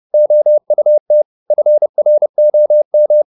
Morse code, heard as a keyed tone, about 30 wpm, 600 Hz. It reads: OUT FROM